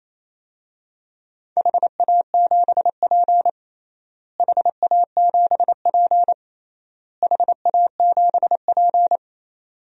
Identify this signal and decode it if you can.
Morse code, heard as a keyed tone, about 28 wpm, 700 Hz. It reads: HA7P HA7P HA7P
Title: DXCC-100 Call Signs - 4 Characters - Repeated 3x - 28wpm